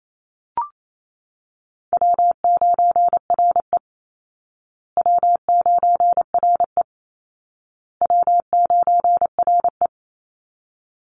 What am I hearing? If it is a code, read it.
Morse code, heard as a keyed tone, about 28 wpm, 700 Hz. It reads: W9RE W9RE W9RE
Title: Easy Call Signs Repeated 3x - 28wpm